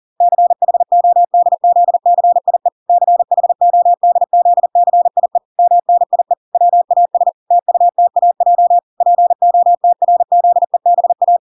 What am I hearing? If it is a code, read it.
Morse code, heard as a keyed tone, about 40 wpm, 700 Hz. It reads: CHODZCIE CHODZCIE MNIE WAS TUTAJ POTRZEBA